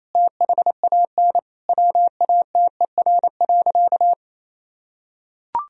Morse code, heard as a keyed tone, about 27 wpm, 700 Hz. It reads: THAN WATER.
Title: Sentences from Top 1000 Words - Review Part 4 - 28wpm